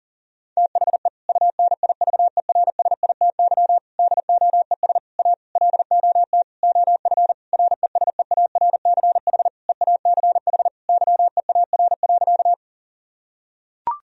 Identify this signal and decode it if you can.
Morse code, heard as a keyed tone, about 40 wpm, 700 Hz. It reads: THE UNIVERSITY DOES A LOT OF RESEARCH EACH YEAR.
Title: Sentences from Top 1000 Words - Review Part 4 - 40wpm